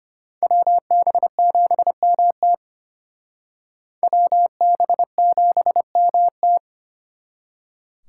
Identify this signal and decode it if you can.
Morse code, heard as a keyed tone, about 32 wpm, 700 Hz. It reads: WB7MT WB7MT